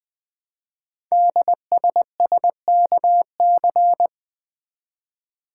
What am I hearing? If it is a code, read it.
Morse code, heard as a keyed tone, about 20 wpm, 700 Hz. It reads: DSSKC
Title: Code Groups - 5 Letters - 20wpm